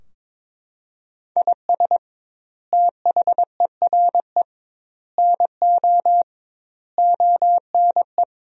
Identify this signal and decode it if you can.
Morse code, heard as a keyed tone, about 21 wpm, 700 Hz. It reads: IS THERE NO ONE
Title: 2x Word-Spacing - Sentences from Top 200 Words - 22wpm